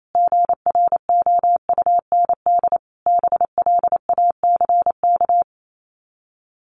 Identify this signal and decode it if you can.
Morse code, heard as a keyed tone, about 28 wpm, 700 Hz. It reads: GROUND BLACK
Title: Sets of 5 Words Encore 28wpm